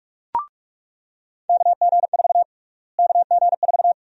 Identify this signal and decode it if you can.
Morse code, heard as a keyed tone, about 45 wpm, 700 Hz. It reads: KG4 KG4
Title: USA Call Sign Prefixes - Repeated 2x - 45wpm